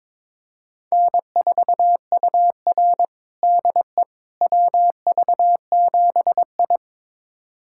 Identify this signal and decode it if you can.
Morse code, heard as a keyed tone, about 22 wpm, 700 Hz. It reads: N4UR DE WV7I
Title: USA Callsign DE Callsign - 4 Characters - 22wpm